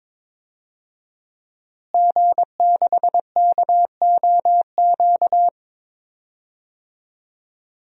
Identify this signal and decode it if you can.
Morse code, heard as a keyed tone, about 22 wpm, 700 Hz. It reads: G6KOQ
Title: DXCC-100 Call Signs - 5 Characters - Repeated 2x - 22wpm